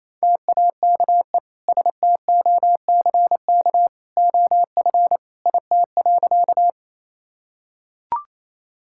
Morse code, heard as a keyed tone, about 28 wpm, 700 Hz. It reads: TAKE STOCK OF IT.
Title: Sentences from Top 1000 Words - Review Part 5 - 28wpm